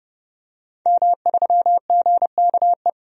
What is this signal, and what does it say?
Morse code, heard as a keyed tone, about 32 wpm, 700 Hz. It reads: M3GKE
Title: DXCC-100 Call Signs - 5 Characters - 30wpm